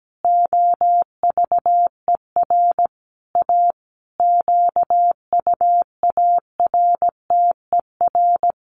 Morse code, heard as a keyed tone, about 17 wpm, 700 Hz. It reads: OVER A QUARTER
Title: Sentences from Top 1300 Words 17wpm